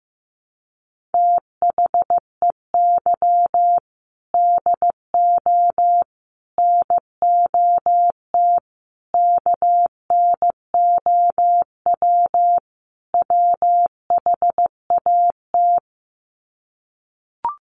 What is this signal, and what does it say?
Morse code, heard as a keyed tone, about 15 wpm, 700 Hz. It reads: THEY DO NOT KNOW WHAT
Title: Sets of 5 Words Encore 15wpm